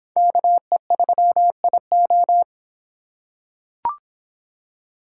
Morse code, heard as a keyed tone, about 24 wpm, 700 Hz. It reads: KE3IO